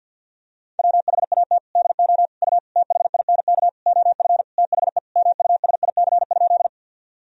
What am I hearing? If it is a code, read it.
Morse code, heard as a keyed tone, about 51 wpm, 700 Hz. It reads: WHAT DO U THINK OF THE MUSIC?
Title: Sentences from Top 300 Words and Abbreviations 50wpm